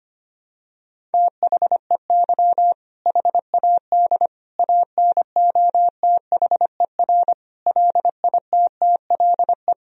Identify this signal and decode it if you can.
Morse code, heard as a keyed tone, about 25 wpm, 700 Hz. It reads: THEY HAD ANOTHER LITTLE
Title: Sentences from Top 1000 Words - Review Part 1 - 25wpm